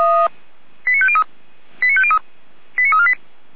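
Five-tone selective call (ZVEI), heard as quick runs of five tones, tones of about 70 ms. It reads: 89503 89503 89348
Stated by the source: Five-tone ZVEI selective call 89014, received by SDR.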